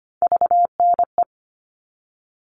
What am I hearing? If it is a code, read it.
Morse code, heard as a keyed tone, about 25 wpm, 700 Hz. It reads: VNE
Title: All Cut Numbers - 3-Digit 25wpm